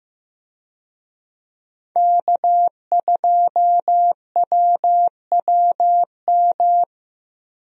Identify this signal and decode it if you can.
Morse code, heard as a keyed tone, about 15 wpm, 700 Hz. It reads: K2WWM
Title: USA Call Signs - 5 Characters - Repeated 2x - 15wpm